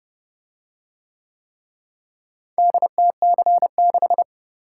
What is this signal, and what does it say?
Morse code, heard as a keyed tone, about 26 wpm, 700 Hz. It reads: DTC6